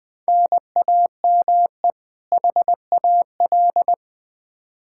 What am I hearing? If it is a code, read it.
Morse code, heard as a keyed tone, about 20 wpm, 700 Hz. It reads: NAME HAL
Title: Practice QSO - Exchange 1 - Beginning - 20wpm